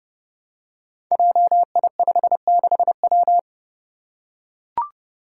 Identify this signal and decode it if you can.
Morse code, heard as a keyed tone, about 32 wpm, 700 Hz. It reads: JI56W
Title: Five Challenging Letter-Number Combinations 30wpm